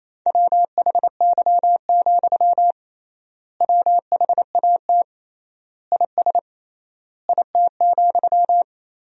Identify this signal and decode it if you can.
Morse code, heard as a keyed tone, about 27 wpm, 700 Hz. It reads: WHY, WHAT IS IT,
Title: 3x Word-Spacing - Sentences from Top 200 Words - 28wpm